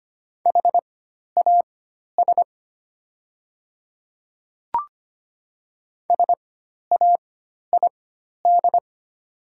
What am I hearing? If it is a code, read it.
Morse code, heard as a keyed tone, about 25 wpm, spaced as for 12 wpm, 700 Hz. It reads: HAS SAID
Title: 4x Character Spacing - Top 100 Words - 25wpm